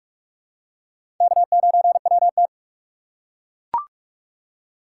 Morse code, heard as a keyed tone, about 41 wpm, 700 Hz. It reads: K9WT
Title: USA Call Signs - 4 Characters - 8x Character Spacing - 45wpm